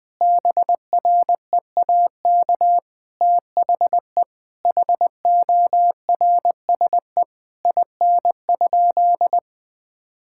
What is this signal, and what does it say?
Morse code, heard as a keyed tone, about 20 wpm, 700 Hz. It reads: BREAK THE HORSE IN?
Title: Sentences from Top 1000 Words - Review Part 6 - 20wpm